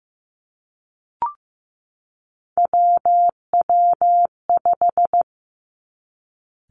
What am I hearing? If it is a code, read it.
Morse code, heard as a keyed tone, about 15 wpm, 700 Hz. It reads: WW5
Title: Three Challenging Letter-Number Combinations 15wpm